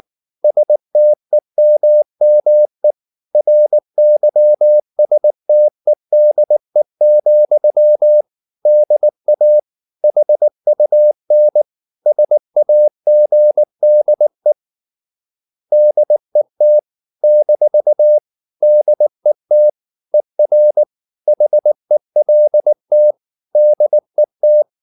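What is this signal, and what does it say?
Morse code, heard as a keyed tone, about 19 wpm, 600 Hz. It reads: STEMME RYSTEDE, DA HUN SAGDE DET - DET ER HELT DET
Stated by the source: Slow (QRS) Morse practice at 18 wpm, CW med Gnister.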